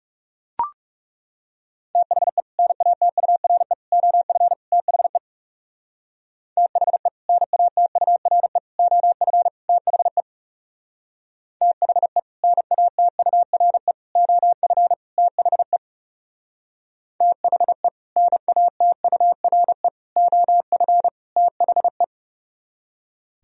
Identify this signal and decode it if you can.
Morse code, heard as a keyed tone, about 37 wpm, 700 Hz. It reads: THE NATURE OF THE THE NATURE OF THE THE NATURE OF THE THE NATURE OF THE
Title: Speed-Racing - Sets of 4 Words 30wpm